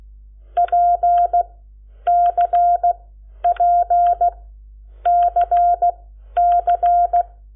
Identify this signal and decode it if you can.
Morse code, heard as a keyed tone, about 16 wpm, spaced as for 10 wpm, 700 Hz. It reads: PCPCC
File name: Lecciones / CLASE Nº7